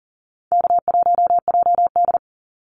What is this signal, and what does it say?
Morse code, heard as a keyed tone, about 42 wpm, 700 Hz. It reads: K1JD
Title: NA SOTA Activator Callsigns 40wpm